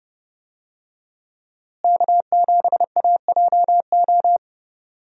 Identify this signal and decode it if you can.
Morse code, heard as a keyed tone, about 30 wpm, 700 Hz. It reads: K7AJO
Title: USA Call Signs - 5 Characters - Repeated 3x - 30wpm